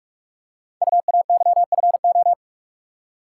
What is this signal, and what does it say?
Morse code, heard as a keyed tone, about 44 wpm, 700 Hz. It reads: UAYFO
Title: Code Groups - 5 Letters - 45wpm